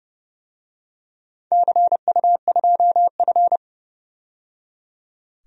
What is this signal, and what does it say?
Morse code, heard as a keyed tone, about 30 wpm, 700 Hz. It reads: CU2F
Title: DXCC-100 Call Signs - 4 Characters - 30wpm